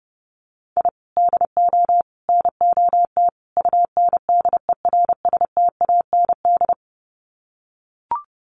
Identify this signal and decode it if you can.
Morse code, heard as a keyed tone, about 30 wpm, 700 Hz. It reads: I DO NOT UNDERSTAND
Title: Sets of 4 Words 30wpm